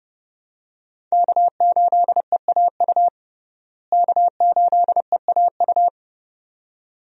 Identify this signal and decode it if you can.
Morse code, heard as a keyed tone, about 30 wpm, 700 Hz. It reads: K8EAU K8EAU
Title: USA Call Signs - 5 Characters - Repeated 2x - 30wpm